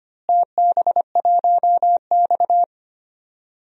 Morse code, heard as a keyed tone, about 25 wpm, 700 Hz. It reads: TB1X
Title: DXCC-100 Call Signs - 4 Characters - Repeated 2x - 25wpm